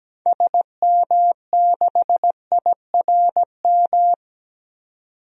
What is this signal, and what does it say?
Morse code, heard as a keyed tone, about 17 wpm, 700 Hz. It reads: SM6IRM
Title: DXCC-100 Call Signs - 6 Characters - 17wpm